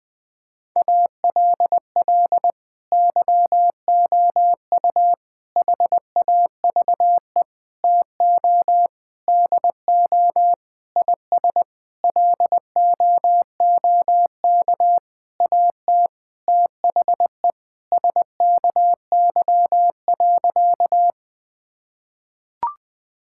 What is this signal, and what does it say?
Morse code, heard as a keyed tone, about 20 wpm, 700 Hz. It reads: ALL YOU HAVE TO DO IS LOOK AT THE SKY.
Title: Sentences with 9-Word Phrases - 20wpm